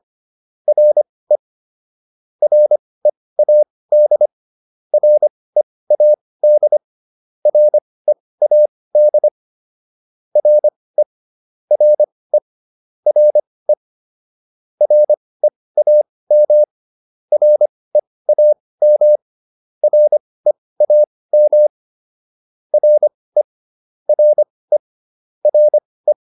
Morse code, heard as a keyed tone, about 25 wpm, spaced as for 18 wpm, 600 Hz. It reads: RE READ READ READ RE RE RE REAM REAM REAM RE RE RE
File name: RE4 18